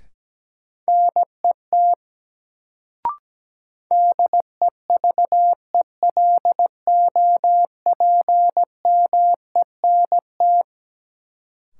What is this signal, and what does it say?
Morse code, heard as a keyed tone, about 17 wpm, 700 Hz. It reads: NET DEVELOPMENT
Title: Sentences from Top 1400 Words - New Word Drill Alt - 17wpm